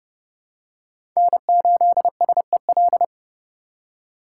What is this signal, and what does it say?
Morse code, heard as a keyed tone, about 30 wpm, 700 Hz. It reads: N8SEL
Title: USA Call Signs - 5 Characters - 4x Character Spacing - 30wpm